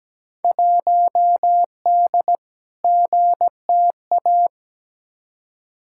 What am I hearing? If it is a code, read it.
Morse code, heard as a keyed tone, about 17 wpm, 700 Hz. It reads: 1D GTA
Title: Field Day Exchanges - 17wpm